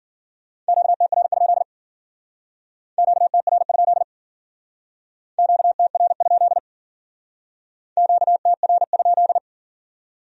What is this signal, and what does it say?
Morse code, heard as a keyed tone, about 48 wpm, 700 Hz. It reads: QTR? QTR? QTR? QTR?
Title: Speed-Racing Q-Codes - 40wpm